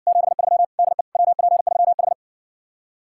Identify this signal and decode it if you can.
Morse code, heard as a keyed tone, about 60 wpm, 700 Hz. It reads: 73 DE WP3H